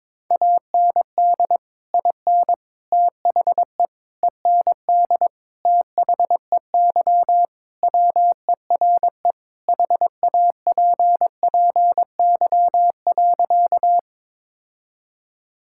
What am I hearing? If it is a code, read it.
Morse code, heard as a keyed tone, about 22 wpm, 700 Hz. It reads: AND IN THE END THEY WERE HAPPY.
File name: practice / 22wpm / Sentences with 2-Word Phrases - 22wpm